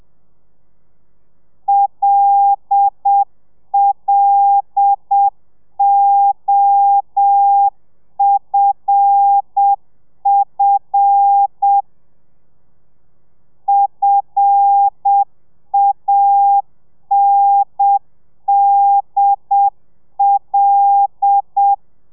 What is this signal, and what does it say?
Morse code, heard as a keyed tone, about 7 wpm, 800 Hz. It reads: LLOFF FANDL
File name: telegrafia / i2tfj / Lez5 30